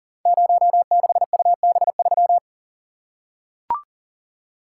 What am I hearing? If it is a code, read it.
Morse code, heard as a keyed tone, about 39 wpm, 690 Hz. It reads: 06UB3
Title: Code Groups - 5 Letter:Numbers - 40wpm